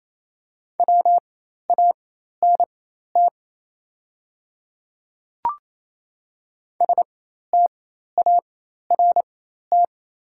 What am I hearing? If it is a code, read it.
Morse code, heard as a keyed tone, about 28 wpm, spaced as for 13 wpm, 690 Hz. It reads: WANT START
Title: 4x Character Spacing - Top 300 Words - 28wpm